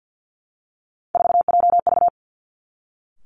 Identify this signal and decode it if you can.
Morse code, heard as a keyed tone, about 50 wpm, 700 Hz. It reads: 4JV
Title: Three Challenging Letter-Number Combinations 50wpm